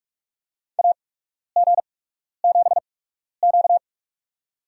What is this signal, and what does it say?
Morse code, heard as a keyed tone, about 45 wpm, spaced as for 12 wpm, 690 Hz. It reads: AG7Q
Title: USA Call Signs - 4 Characters - 8x Character Spacing - 45wpm